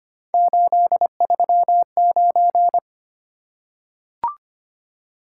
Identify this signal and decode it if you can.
Morse code, heard as a keyed tone, about 25 wpm, 700 Hz. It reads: 839
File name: Three Letter-Number Combinations - 25wpm